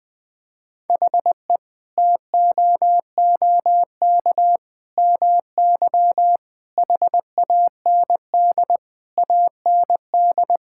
Morse code, heard as a keyed tone, about 20 wpm, 700 Hz. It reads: HE TOOK MY HAND AND
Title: Sentences from Top 1000 Words - Review Part 6 - 20wpm